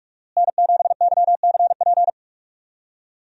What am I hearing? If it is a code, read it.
Morse code, heard as a keyed tone, about 45 wpm, 700 Hz. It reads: N7YCP
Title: USA Call Signs - 5 Characters - Repeated 2x - 45wpm